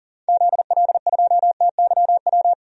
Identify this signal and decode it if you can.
Morse code, heard as a keyed tone, about 40 wpm, 700 Hz. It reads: ZL2TYW
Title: DXCC-100 Call Signs - 6 Characters - Repeated 2x - 40wpm